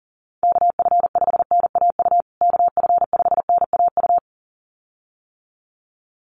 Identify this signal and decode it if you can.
Morse code, heard as a keyed tone, about 37 wpm, 700 Hz. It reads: KF5NAU KF5NAU
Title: USA Call Signs Repeated 2x 40wpm